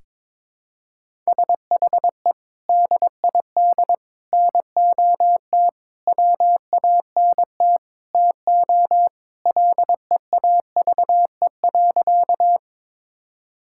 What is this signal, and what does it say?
Morse code, heard as a keyed tone, about 22 wpm, 700 Hz. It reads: SHE DID NOT WANT TO LEAVE.
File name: practice / 22wpm / Sentences with 4-Word Phrases - 22wpm